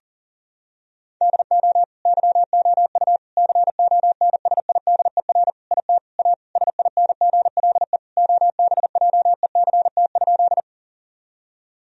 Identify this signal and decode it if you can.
Morse code, heard as a keyed tone, about 40 wpm, 700 Hz. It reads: DO YOU CONSIDER IT A SINGLE OBJECT?
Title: Sentences from Top 1000 Words - Review Part 4 - 40wpm